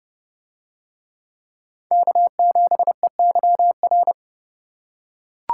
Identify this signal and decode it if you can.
Morse code, heard as a keyed tone, about 30 wpm, 700 Hz. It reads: K7EYR